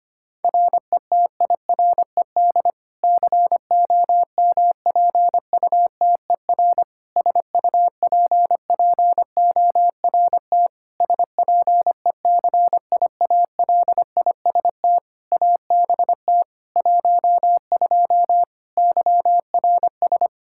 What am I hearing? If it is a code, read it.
Morse code, heard as a keyed tone, about 25 wpm, 700 Hz. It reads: RETIRED COMPUTER SUPPORT SPECIALIST ABT 12 YRS
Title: Practice QSO - Exchange 3 - More Info - 25wpm